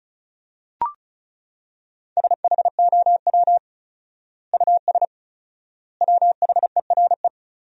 Morse code, heard as a keyed tone, about 34 wpm, 700 Hz. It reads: SHOW US WHERE
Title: 4x Word-Spacing - Sentences from Top 200 Words - 35wpm